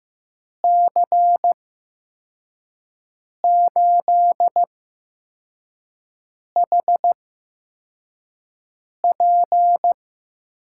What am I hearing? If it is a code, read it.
Morse code, heard as a keyed tone, about 16 wpm, spaced as for 4 wpm, 700 Hz. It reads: C8HP
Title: USA Call Signs - 5 Characters - 8x Character Spacing - 15wpm